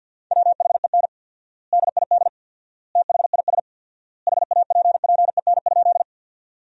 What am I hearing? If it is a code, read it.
Morse code, heard as a keyed tone, about 50 wpm, 700 Hz. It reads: WHEN DID THIS HAPPEN?
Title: 4x Word-Spacing - Sentences from Top 400 Words - 50wpm